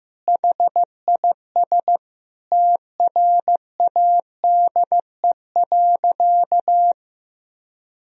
Morse code, heard as a keyed tone, about 15 wpm, 700 Hz. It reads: HIS TRADE.
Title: Sentences from Top 1000 Words - Review Part 4 - 15wpm